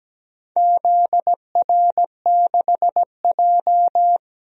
Morse code, heard as a keyed tone, about 17 wpm, 700 Hz. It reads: ZR6J
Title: DXCC-100 Call Signs - 4 Characters - 17wpm